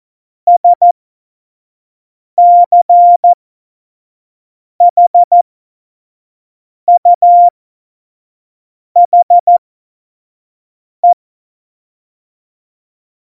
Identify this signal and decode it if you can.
Morse code, heard as a keyed tone, about 14 wpm, spaced as for 5 wpm, 700 Hz. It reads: SCHUHE
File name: Morsekurs / Witze / Witz 22. Mai 2018, 25 F70